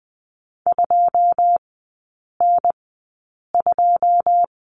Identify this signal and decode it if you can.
Morse code, heard as a keyed tone, about 20 wpm, spaced as for 8 wpm, 700 Hz. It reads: 2N2
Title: 093 Lesson 16e - Letter c - Three Characters 20wpm